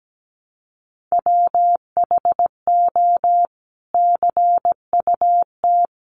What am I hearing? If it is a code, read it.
Morse code, heard as a keyed tone, about 17 wpm, 700 Hz. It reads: WHO CUT
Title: Sentences from Top 1300 Words 17wpm